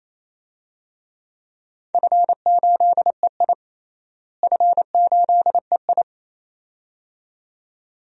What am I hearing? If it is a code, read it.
Morse code, heard as a keyed tone, about 28 wpm, 700 Hz. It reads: F8EI F8EI
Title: DXCC-100 Call Signs - 4 Characters - Repeated 2x - 28wpm